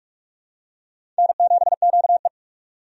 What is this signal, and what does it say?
Morse code, heard as a keyed tone, about 46 wpm, 700 Hz. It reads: N7QE
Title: USA Call Signs - 4 Characters - 45wpm